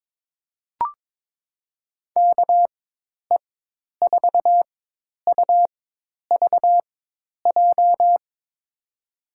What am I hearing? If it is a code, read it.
Morse code, heard as a keyed tone, about 22 wpm, spaced as for 10 wpm, 700 Hz. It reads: KE4UVJ
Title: USA Call Signs - 6 Characters - 4x Character Spacing - 22wpm